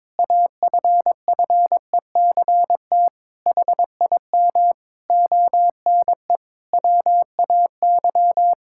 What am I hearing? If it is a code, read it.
Morse code, heard as a keyed tone, about 22 wpm, 700 Hz. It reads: AFFECT HIM ONE WAY
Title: Sentences from Top 1000 Words - Review Part 6 - 22wpm